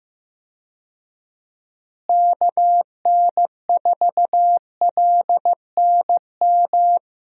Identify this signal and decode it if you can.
Morse code, heard as a keyed tone, about 15 wpm, 700 Hz. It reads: KN4LNM
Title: USA Call Signs - 6 Characters - Repeated 2x - 15wpm